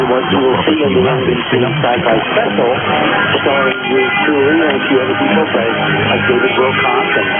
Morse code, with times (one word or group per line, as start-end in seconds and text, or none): none